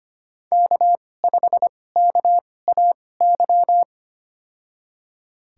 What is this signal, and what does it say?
Morse code, heard as a keyed tone, about 25 wpm, spaced as for 18 wpm, 700 Hz. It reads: K5KAY
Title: USA Call Signs - 5 Characters - 2x Character Spacing - 25wpm